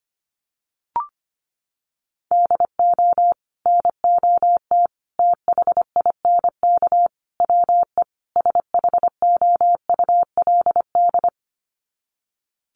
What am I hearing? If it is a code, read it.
Morse code, heard as a keyed tone, about 25 wpm, 700 Hz. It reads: DO NOT THINK WE SHOULD
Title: Sets of 5 Words Encore 25wpm